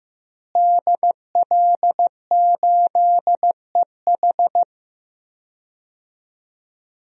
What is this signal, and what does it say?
Morse code, heard as a keyed tone, about 15 wpm, 700 Hz. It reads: DL8EH
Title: DXCC-100 Call Signs - 5 Characters - Repeated 2x - 15wpm